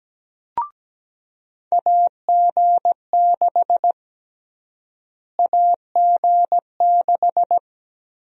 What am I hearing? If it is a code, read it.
Morse code, heard as a keyed tone, about 17 wpm, 700 Hz. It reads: AG6 AG6
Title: USA Call Sign Prefixes - Repeated 2x - 17wpm